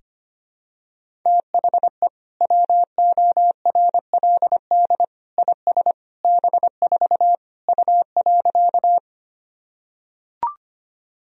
Morse code, heard as a keyed tone, about 25 wpm, 700 Hz. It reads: THE WORLD IS B4 U.
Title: Sentences from Top 200 Words and Abbreviations 25wpm